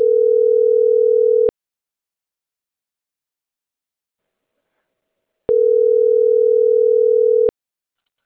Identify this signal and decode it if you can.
Ringback tone (North America), 440 + 480 Hz.